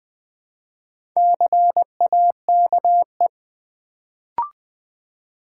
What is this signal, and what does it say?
Morse code, heard as a keyed tone, about 20 wpm, 700 Hz. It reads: CAKE
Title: Build A Word - 1 Letter At A Time - 4-Letter Words - 20wpm